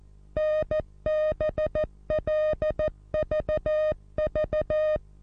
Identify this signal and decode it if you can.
Morse code, heard as a keyed tone, about 14 wpm, 620 Hz. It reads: NBLVV
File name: repos / Morsekurs / DJ4UF Morsen27